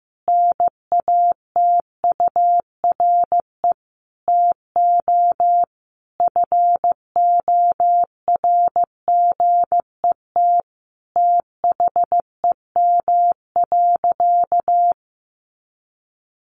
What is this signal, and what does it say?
Morse code, heard as a keyed tone, about 15 wpm, 700 Hz. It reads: NATURE TO FORGET THEM.
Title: Sentences from Top 1200 Words 15wpm